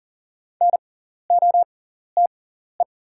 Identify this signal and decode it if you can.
Morse code, heard as a keyed tone, about 40 wpm, spaced as for 14 wpm, 690 Hz. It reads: NOTE
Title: 6x Character Spacing - Top 500 Words - 40wpm